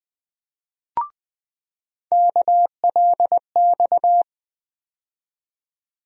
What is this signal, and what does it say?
Morse code, heard as a keyed tone, about 21 wpm, 700 Hz. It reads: KLX
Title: USA Call Sign Postfixes - 3 Letters 20wpm